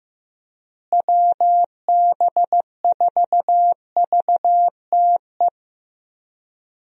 Morse code, heard as a keyed tone, about 15 wpm, 700 Hz. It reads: WB4VTE
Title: USA Call Signs - 6 Characters - Repeated 3x - 15wpm